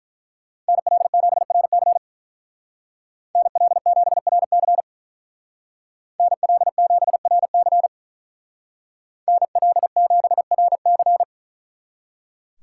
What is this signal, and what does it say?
Morse code, heard as a keyed tone, about 42 wpm, 700 Hz. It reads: NL7RC NL7RC NL7RC NL7RC
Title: Speed Racing - USA 2x2 Call Signs 35wpm